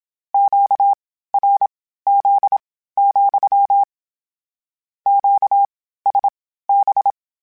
Morse code, heard as a keyed tone, about 27 wpm, spaced as for 15 wpm, 800 Hz. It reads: QRZ, QSB
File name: Q Calls / Q Calls-mp3s / Q CALLS SPSL @25